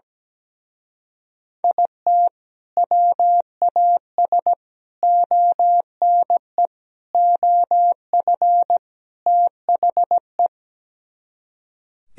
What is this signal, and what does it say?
Morse code, heard as a keyed tone, about 17 wpm, 700 Hz. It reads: IT WAS ONE OF THE
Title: Speed-Racing - Sets of 5 Words 17wpm